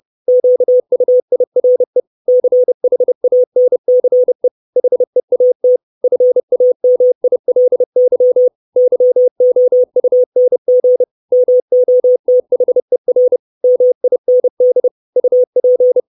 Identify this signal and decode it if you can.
Morse code, heard as a keyed tone, about 30 wpm, 500 Hz. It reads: QUIRE CHANCE HEAT FAMILY YOUNG MOTHER MIND UP
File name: 632 Words 30WPM